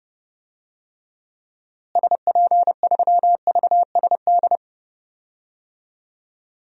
Morse code, heard as a keyed tone, about 30 wpm, 700 Hz. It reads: SP3VSD
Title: DXCC-100 Call Signs - 6 Characters - Repeated 3x - 30wpm